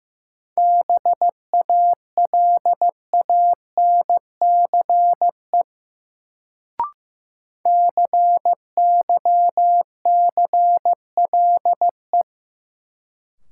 Morse code, heard as a keyed tone, about 15 wpm, 700 Hz. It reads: BALANCE CYCLE
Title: Sentences from Top 900 Words - New Word Drill - 15wpm